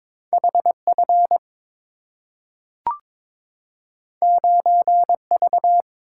Morse code, heard as a keyed tone, about 22 wpm, 690 Hz. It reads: HF 9V
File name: practice / 22wpm / Two Letter-Number Combinations - 22wpm